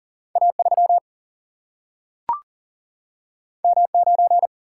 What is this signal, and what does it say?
Morse code, heard as a keyed tone, about 40 wpm, 700 Hz. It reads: A 3 M 9
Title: Single-Letter Single-Number 40wpm